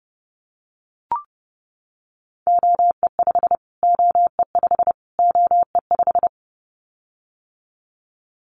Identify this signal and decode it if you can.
Morse code, heard as a keyed tone, about 30 wpm, 700 Hz. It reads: OE5 OE5 OE5
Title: Callsign Prefixes for Top 25 DXCC - Repeated 3x - 30wpm